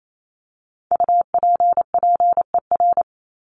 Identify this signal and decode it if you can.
Morse code, heard as a keyed tone, about 28 wpm, 700 Hz. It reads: UPPER
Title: Top 1751-2000 Words 28wpm